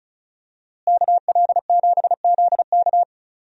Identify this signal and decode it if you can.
Morse code, heard as a keyed tone, about 36 wpm, 700 Hz. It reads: KL7ZK
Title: USA Call Signs - 5 Characters - Repeated 3x - 35wpm